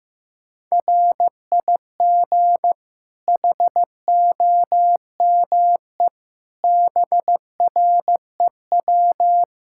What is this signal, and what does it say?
Morse code, heard as a keyed tone, about 16 wpm, 700 Hz. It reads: RIG HOME BREW
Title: Practice QSO - Exchange 2 - Info - 15wpm